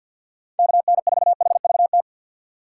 Morse code, heard as a keyed tone, about 63 wpm, 700 Hz. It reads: KN4HVT